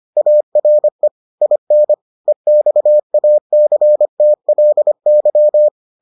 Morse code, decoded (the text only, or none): ARE IN EXACTLY